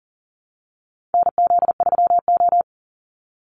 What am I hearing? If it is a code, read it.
Morse code, heard as a keyed tone, about 40 wpm, 700 Hz. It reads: NZ3O
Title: Call Signs - 40wpm